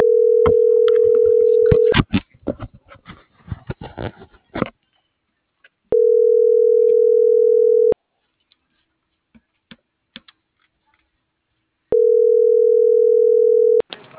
Ringback tone (North America), 440 + 480 Hz.